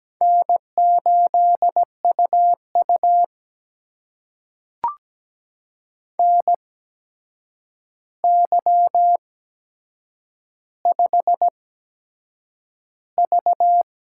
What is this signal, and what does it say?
Morse code, heard as a keyed tone, about 17 wpm, spaced as for 7 wpm, 700 Hz. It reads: N8UU NY5V
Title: USA Call Signs - 4 Characters - 8x Character Spacing - 17wpm